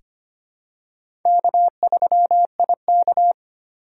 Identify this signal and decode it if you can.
Morse code, heard as a keyed tone, about 25 wpm, 700 Hz. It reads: K3IK
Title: USA Call Signs - 4 Characters - 2x Character Spacing - 25wpm